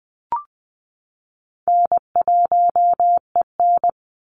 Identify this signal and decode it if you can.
Morse code, heard as a keyed tone, about 20 wpm, 700 Hz. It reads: N1EN
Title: Call Signs - 20wpm